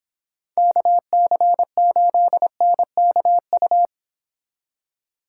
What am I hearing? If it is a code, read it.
Morse code, heard as a keyed tone, about 26 wpm, 700 Hz. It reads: KC8NKU